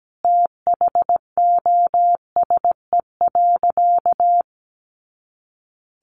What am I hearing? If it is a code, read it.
Morse code, heard as a keyed tone, about 17 wpm, 700 Hz. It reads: THOSE.
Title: A Sound Of Thunder 17wpm